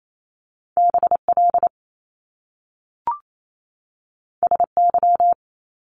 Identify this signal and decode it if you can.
Morse code, heard as a keyed tone, about 28 wpm, 700 Hz. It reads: BL SY
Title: Top Two Letter Combinations in Words 28wpm